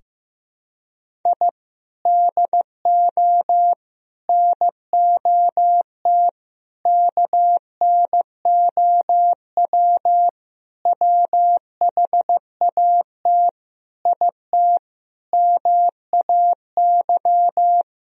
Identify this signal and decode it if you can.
Morse code, heard as a keyed tone, about 15 wpm, 700 Hz. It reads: I DO NOT KNOW WHAT IT MAY
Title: Sentences from Top 1000 Words - Review Part 2 - 15wpm